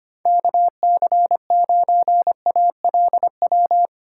Morse code, heard as a keyed tone, about 30 wpm, 700 Hz. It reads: KC9ALW
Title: Speed Racing - USA Call Signs 25wpm